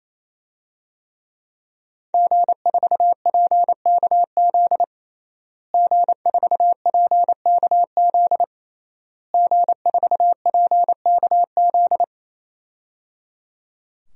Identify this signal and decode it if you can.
Morse code, heard as a keyed tone, about 28 wpm, 700 Hz. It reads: G4PKZ G4PKZ G4PKZ
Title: DXCC-100 Call Signs - 5 Characters - Repeated 3x - 28wpm